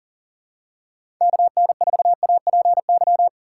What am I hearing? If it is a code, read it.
Morse code, heard as a keyed tone, about 39 wpm, 700 Hz. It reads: KN4APY
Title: USA Call Signs - 6 Characters - 6x Character Spacing - 40wpm